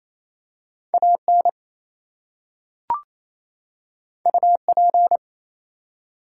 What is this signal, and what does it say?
Morse code, heard as a keyed tone, about 27 wpm, 700 Hz. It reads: AN UP
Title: Two Letter Words - 28wpm